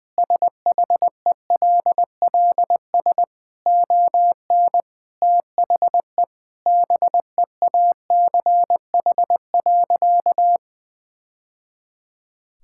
Morse code, heard as a keyed tone, about 21 wpm, 700 Hz. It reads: SHELLS ON THE BEACH.
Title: Easy Sentences with Hints - 20wpm